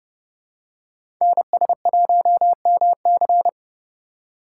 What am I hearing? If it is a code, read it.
Morse code, heard as a keyed tone, about 29 wpm, 700 Hz. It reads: NS1MC
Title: USA Call Signs - 5 Characters - 3x Character Spacing - 30wpm